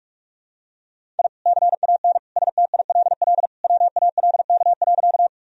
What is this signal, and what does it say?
Morse code, heard as a keyed tone, about 44 wpm, 700 Hz. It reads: I CAN STILL WALK.
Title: Sentences from Top 1000 Words - Review Part 1 - 45wpm